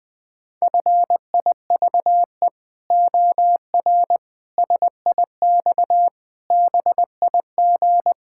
Morse code, heard as a keyed tone, about 20 wpm, 700 Hz. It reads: FIVE OR SIX BIG
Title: Pangram Challenge v2 - 20wpm